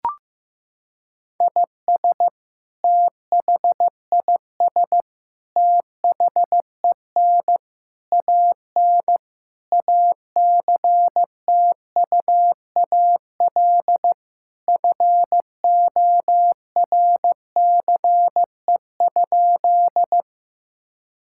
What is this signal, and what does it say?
Morse code, heard as a keyed tone, about 16 wpm, 700 Hz. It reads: IS THIS THEN AN ACTUAL FORCE?
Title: Sentences from Top 1000 Words - Review Part 5 - 15wpm